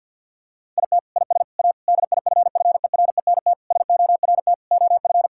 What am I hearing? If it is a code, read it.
Morse code, heard as a keyed tone, about 50 wpm, 700 Hz. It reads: IT IS A DIFFERENT SORT OF
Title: Sentences from Top 1000 Words - Review Part 5 - 50wpm